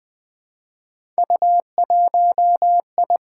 Morse code, heard as a keyed tone, about 20 wpm, 700 Hz. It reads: U1I
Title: Three Letter-Number Combinations - 20wpm